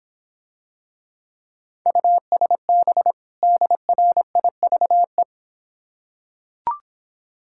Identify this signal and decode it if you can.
Morse code, heard as a keyed tone, about 26 wpm, 700 Hz. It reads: USB DRIVE